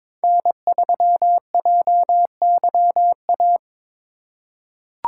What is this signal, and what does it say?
Morse code, heard as a keyed tone, about 22 wpm, 700 Hz. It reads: N3JYA
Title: USA Call Signs - 2x Character Spacing - 22wpm